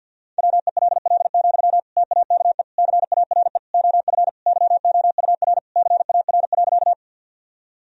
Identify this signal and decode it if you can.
Morse code, heard as a keyed tone, about 52 wpm, 700 Hz. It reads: WELL, TAKE CARE OF YOUR CAR.
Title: Sentences from Top 1000 Words - Review Part 1 - 50wpm